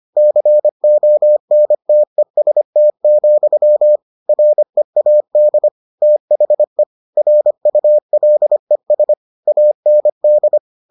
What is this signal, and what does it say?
Morse code, heard as a keyed tone, about 25 wpm, 600 Hz. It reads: CONTEST, READ THE RULES AND